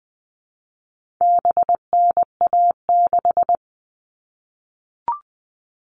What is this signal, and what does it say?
Morse code, heard as a keyed tone, about 20 wpm, 700 Hz. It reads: BNA6
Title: All Cut Numbers - 4-Digit 20wpm